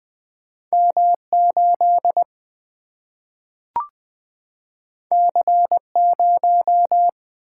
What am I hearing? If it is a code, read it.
Morse code, heard as a keyed tone, about 20 wpm, 700 Hz. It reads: M 8 C 0
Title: Single-Letter Single-Number 20wpm